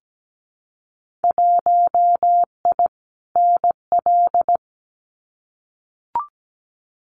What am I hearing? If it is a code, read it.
Morse code, heard as a keyed tone, about 17 wpm, 700 Hz. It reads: 1I NL
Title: Winter Field Day Exchanges 17wpm